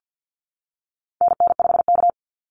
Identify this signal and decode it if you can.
Morse code, heard as a keyed tone, about 48 wpm, 700 Hz. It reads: NN5K
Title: NA SOTA Activator Callsigns 50wpm